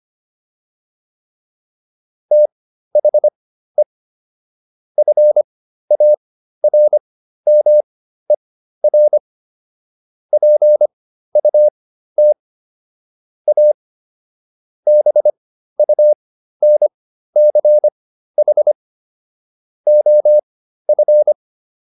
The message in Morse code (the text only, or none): THE FARMER PUT A BUNCH OF